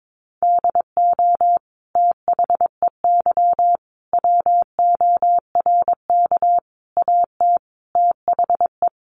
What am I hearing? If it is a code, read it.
Morse code, heard as a keyed tone, about 22 wpm, 700 Hz. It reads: DO THEY WORK AT THE
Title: Sentences from Top 1300 Words 22wpm